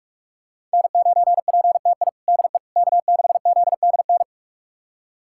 Morse code, heard as a keyed tone, about 46 wpm, 700 Hz. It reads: N9PTI DE K6ZDN
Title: USA Callsign DE Callsign - 5 Characters - 45wpm